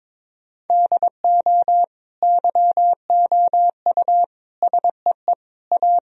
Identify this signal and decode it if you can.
Morse code, heard as a keyed tone, about 21 wpm, 700 Hz. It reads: DO YOU SEE A
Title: Sentences from Top 1000 Words - Review Part 4 - 22wpm